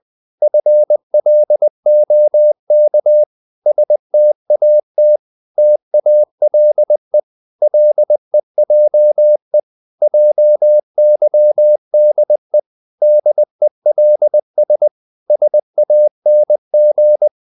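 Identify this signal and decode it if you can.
Morse code, heard as a keyed tone, about 20 wpm, 600 Hz. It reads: FLOK STAT TALE LEJE JYDE DELS SANG
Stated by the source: Morse code practice of short Danish words at 20 wpm, from CW med Gnister.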